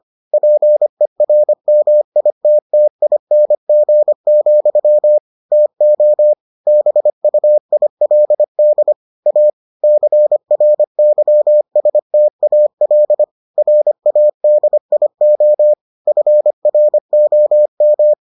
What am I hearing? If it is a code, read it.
Morse code, heard as a keyed tone, about 25 wpm, 600 Hz. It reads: PERMITTING, TO BUILD A CRYSTAL RADIO FROM